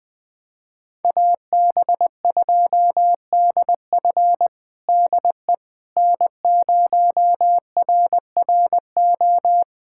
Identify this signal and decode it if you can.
Morse code, heard as a keyed tone, about 20 wpm, 700 Hz. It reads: AB2DF DE N0RRO
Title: USA Callsign DE Callsign 20wpm